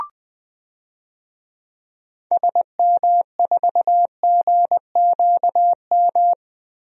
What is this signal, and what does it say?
Morse code, heard as a keyed tone, about 20 wpm, 700 Hz. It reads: SM4GQM
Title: DXCC-100 Call Signs - 6 Characters - 20wpm